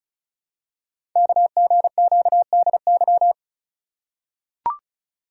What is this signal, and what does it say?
Morse code, heard as a keyed tone, about 36 wpm, 700 Hz. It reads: KGQDY
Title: Code Groups - 5 Letters - 35wpm